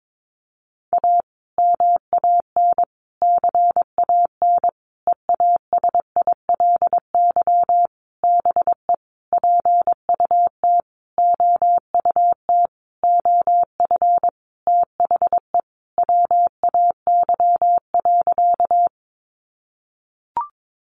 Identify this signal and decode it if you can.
Morse code, heard as a keyed tone, about 22 wpm, 700 Hz. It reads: A MAN CAN EASILY BE PUT OUT OF THE WAY.
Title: Sentences from Top 900 Words 22wpm